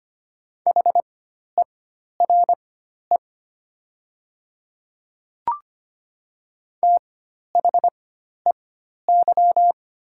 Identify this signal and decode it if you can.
Morse code, heard as a keyed tone, about 25 wpm, spaced as for 12 wpm, 700 Hz. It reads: HERE THEY
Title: 4x Character Spacing - Top 200 Words - 25wpm